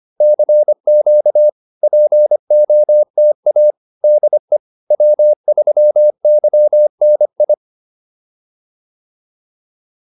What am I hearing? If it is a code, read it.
Morse code, heard as a keyed tone, about 25 wpm, 600 Hz. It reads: CQ POTA DE W3YNI